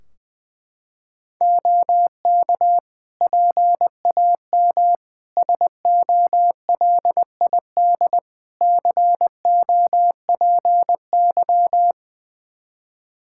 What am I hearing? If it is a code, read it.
Morse code, heard as a keyed tone, about 20 wpm, 700 Hz. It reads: OK PAM SOLID COPY
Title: Practice QSO - Exchange 3 - More Info - 20wpm